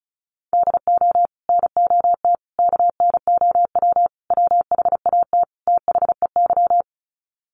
Morse code, heard as a keyed tone, about 34 wpm, 700 Hz. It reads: DO NOT KNOW WHAT THEY
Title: Sets of 5 Words 35wpm